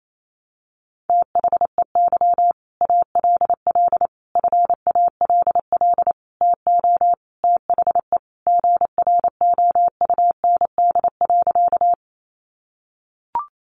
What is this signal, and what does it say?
Morse code, heard as a keyed tone, about 28 wpm, 700 Hz. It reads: THEY ALL FALL TO THE GROUND.
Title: Sentences from Top 500 Words 28wpm